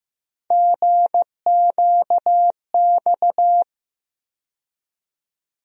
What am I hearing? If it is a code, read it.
Morse code, heard as a keyed tone, about 15 wpm, 700 Hz. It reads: GQX
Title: USA Call Sign Postfixes - 3 Letters 15wpm